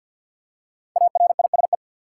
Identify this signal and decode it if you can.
Morse code, heard as a keyed tone, about 53 wpm, 700 Hz. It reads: ARISE
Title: Five Letter Words - 50wpm